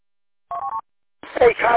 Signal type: none